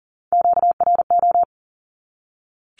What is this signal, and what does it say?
Morse code, heard as a keyed tone, about 40 wpm, 700 Hz. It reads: QRO
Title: Q-Codes 40wpm